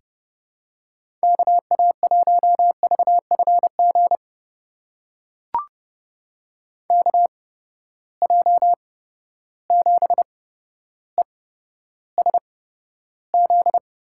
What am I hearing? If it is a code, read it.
Morse code, heard as a keyed tone, about 30 wpm, spaced as for 13 wpm, 700 Hz. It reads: KA1VFG KJ7ESZ
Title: USA Call Signs - 6 Characters - 8x Character Spacing - 30wpm